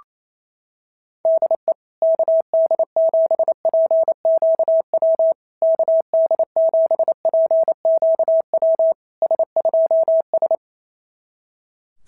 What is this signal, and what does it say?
Morse code, heard as a keyed tone, about 27 wpm, 650 Hz. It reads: DE KD7PQW KD7PQW S2S
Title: SOTA QSO Practice 28wpm